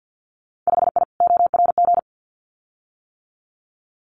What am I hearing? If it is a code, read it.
Morse code, heard as a keyed tone, about 50 wpm, 700 Hz. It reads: 5I ORG
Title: Winter Field Day Exchanges Encore 50wpm